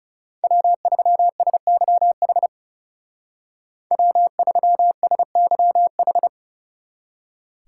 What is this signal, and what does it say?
Morse code, heard as a keyed tone, about 35 wpm, 700 Hz. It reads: W3SYH W3SYH
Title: Speed Racing - USA 1x3 Call Signs 30wpm